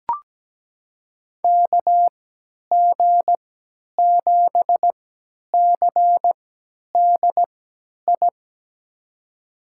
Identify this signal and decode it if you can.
Morse code, heard as a keyed tone, about 17 wpm, spaced as for 10 wpm, 700 Hz. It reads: KG7CDI
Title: USA Call Signs - 6 Characters - 3x Character Spacing - 17wpm